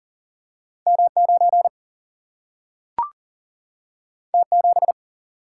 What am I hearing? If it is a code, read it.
Morse code, heard as a keyed tone, about 38 wpm, 700 Hz. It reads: M 9 T 7
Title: Single-Letter Single-Number 40wpm